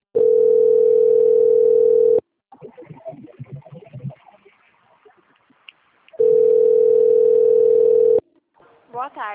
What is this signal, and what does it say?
Ringback tone (North America), 440 + 480 Hz, with speech.
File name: out- - - - - .36418